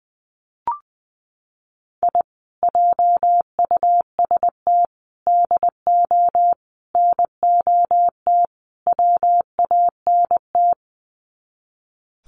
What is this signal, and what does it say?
Morse code, heard as a keyed tone, about 20 wpm, 700 Hz. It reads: I JUST DO NOT WANT
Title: Sets of 5 Words Encore2 20wpm